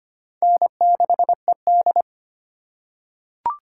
Morse code, heard as a keyed tone, about 25 wpm, 700 Hz. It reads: N6ED
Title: USA Call Signs - 4 Characters - 6x Character Spacing - 25wpm